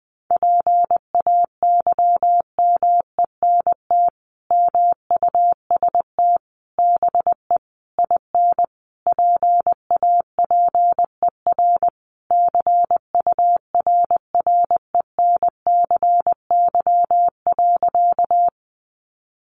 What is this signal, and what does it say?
Morse code, heard as a keyed tone, about 20 wpm, 700 Hz. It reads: PAYMENT MUST BE IN PAPER CURRENCY.
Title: Sentences from Top 1500 Words 20wpm